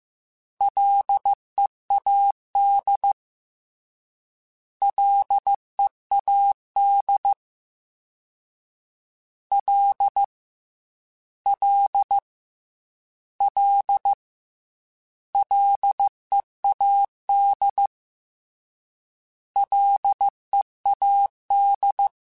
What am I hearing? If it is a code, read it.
Morse code, heard as a keyed tone, about 15 wpm, 800 Hz. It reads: LEAD LEAD L L L LEAD LEAD
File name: F and L / F and L mp3 / F and L Prefixes 3x2 Twice @13